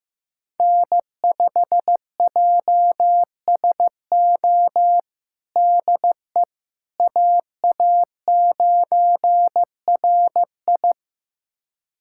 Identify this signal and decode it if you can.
Morse code, heard as a keyed tone, about 16 wpm, 700 Hz. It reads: N5JSO DE AA9RI
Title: USA Callsign DE Callsign - 5 Characters - 15wpm